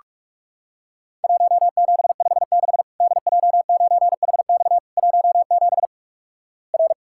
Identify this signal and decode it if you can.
Morse code, heard as a keyed tone, about 48 wpm, 700 Hz. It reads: 1756 DJ9HX 17 R
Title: Worked All Europe DX Contest - QTC Exchange - 45wpm